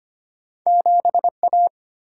Morse code, heard as a keyed tone, about 25 wpm, 690 Hz. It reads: 7A
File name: Two Letter-Number Combinations - 25wpm